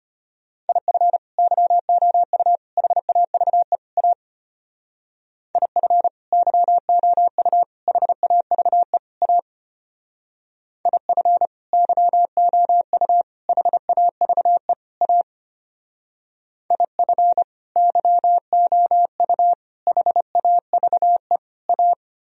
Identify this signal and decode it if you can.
Morse code, heard as a keyed tone, about 32 wpm, 700 Hz. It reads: IF YOU HAVE A IF YOU HAVE A IF YOU HAVE A IF YOU HAVE A